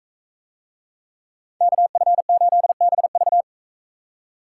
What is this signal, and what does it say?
Morse code, heard as a keyed tone, about 41 wpm, 700 Hz. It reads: KF8BV